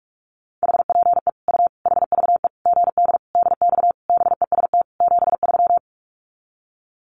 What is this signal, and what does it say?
Morse code, heard as a keyed tone, about 44 wpm, 700 Hz. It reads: HPE U HVE GD DX BEST 73
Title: Common QSO Phrases 45wpm